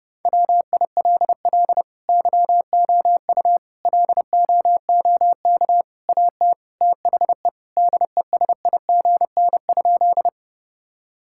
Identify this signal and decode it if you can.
Morse code, heard as a keyed tone, about 32 wpm, 700 Hz. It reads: WILL YOU LOOK AT THE DESIGN?
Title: Sentences from Top 1000 Words - Review Part 5 - 30wpm